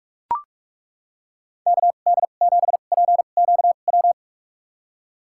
Morse code, heard as a keyed tone, about 41 wpm, spaced as for 31 wpm, 700 Hz. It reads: KD7PQW
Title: USA Call Signs - 6 Characters - 2x Character Spacing - 45wpm